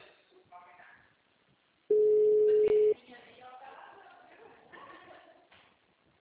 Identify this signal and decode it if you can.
Ringback tone (Europe, 425 Hz), with speech.